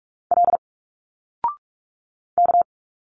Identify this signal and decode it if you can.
Morse code, heard as a keyed tone, about 46 wpm, 700 Hz. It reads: L K
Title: Single Letters 45wpm